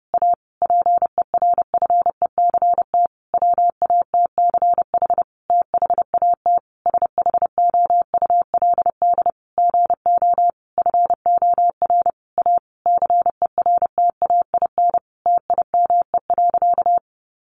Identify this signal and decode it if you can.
Morse code, heard as a keyed tone, about 30 wpm, 700 Hz. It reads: A PERFECT WATCH THAT SHOULD GO FOR A CERTAIN TIME.
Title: Sentences from Top 1000 Words 30wpm